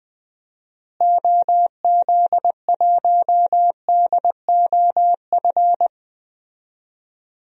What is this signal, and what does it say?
Morse code, heard as a keyed tone, about 20 wpm, 700 Hz. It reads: OZ1DOF
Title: DXCC-100 Call Signs - 6 Characters - Repeated 2x - 20wpm